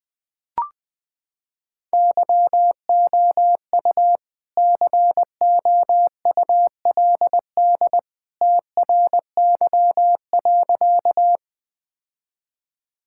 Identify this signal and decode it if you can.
Morse code, heard as a keyed tone, about 21 wpm, 700 Hz. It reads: YOU COULD TRY.
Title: Sentences with 2-Word Phrases - 20wpm